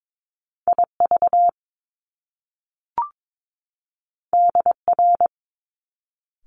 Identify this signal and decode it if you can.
Morse code, heard as a keyed tone, about 22 wpm, 700 Hz. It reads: IV DR
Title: Top Two Letter Combinations in Words 22wpm